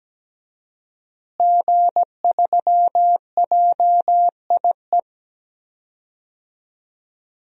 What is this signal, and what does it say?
Morse code, heard as a keyed tone, about 17 wpm, 700 Hz. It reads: G3JIE
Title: DXCC-100 Call Signs - 5 Characters - Repeated 3x - 17wpm